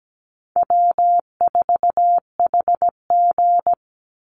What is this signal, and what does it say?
Morse code, heard as a keyed tone, about 17 wpm, 700 Hz. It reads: W4HG
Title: Easy Call Signs Repeated 2x - 17wpm